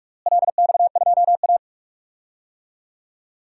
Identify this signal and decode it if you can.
Morse code, heard as a keyed tone, about 46 wpm, 700 Hz. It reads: LX2A
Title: DXCC-100 Call Signs - 4 Characters - Repeated 2x - 45wpm